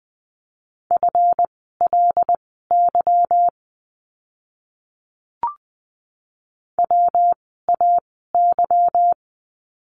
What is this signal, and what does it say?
Morse code, heard as a keyed tone, about 20 wpm, spaced as for 14 wpm, 700 Hz. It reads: FLY WAY
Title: 146 Lesson 24f - Letter y - Words 20wpm